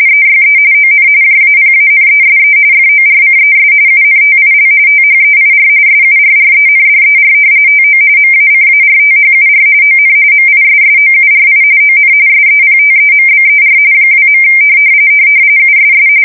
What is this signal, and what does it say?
RTTY, about 45.45 baud. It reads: EYE ON JAPANESE FORCES UNTIL THEIR SURRENDER TO MACARTHUR ON MISSOURI ON SEPTEMBER 2. USS
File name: fsk170 / battleshipnj